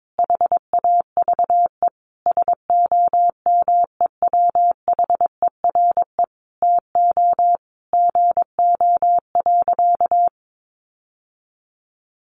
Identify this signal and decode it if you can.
Morse code, heard as a keyed tone, about 22 wpm, 700 Hz. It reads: HAVE SOMEWHERE TO GO.
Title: Sentences from Top 1400 Words 22wpm